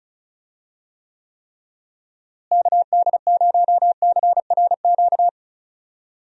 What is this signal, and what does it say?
Morse code, heard as a keyed tone, about 36 wpm, 700 Hz. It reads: KD0CRQ
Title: USA Call Signs - 6 Characters - Repeated 3x - 35wpm